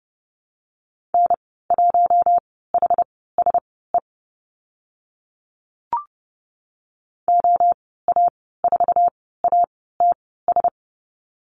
Morse code, heard as a keyed tone, about 30 wpm, spaced as for 17 wpm, 700 Hz. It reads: N1HSE OA4ATS
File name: Lesson 14g - Number 2 - Call Signs 30wpm